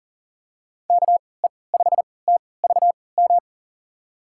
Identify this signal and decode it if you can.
Morse code, heard as a keyed tone, about 40 wpm, spaced as for 23 wpm, 700 Hz. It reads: KE5TVM
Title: USA Call Signs - 6 Characters - 3x Character Spacing - 40wpm